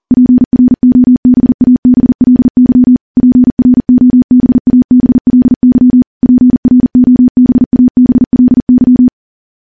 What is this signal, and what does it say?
Morse code, heard as a keyed tone, about 40 wpm, 250 Hz. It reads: PROBABLY PROBABLY PROBABLY
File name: morse / morse-training-40WPM